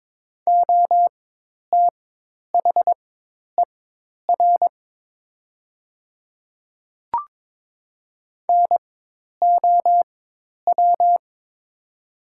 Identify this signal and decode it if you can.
Morse code, heard as a keyed tone, about 22 wpm, spaced as for 10 wpm, 700 Hz. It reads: OTHER NOW
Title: 4x Character Spacing - Top 200 Words - 22wpm